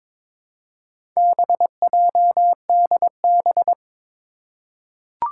Morse code, heard as a keyed tone, about 22 wpm, 700 Hz. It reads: BJDB
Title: Four Challenging Letter-Number Combinations 22wpm